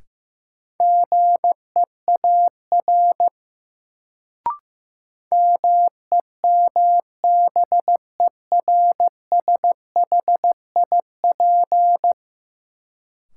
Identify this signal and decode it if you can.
Morse code, heard as a keyed tone, about 15 wpm, 700 Hz. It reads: GEAR MEMBERSHIP
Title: Sentences from Top 1400 Words - New Word Drill Alt - 15wpm